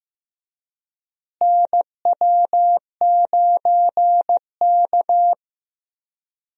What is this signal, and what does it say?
Morse code, heard as a keyed tone, about 15 wpm, 700 Hz. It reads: NW9K
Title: USA Call Signs - 4 Characters - Repeated 2x - 15wpm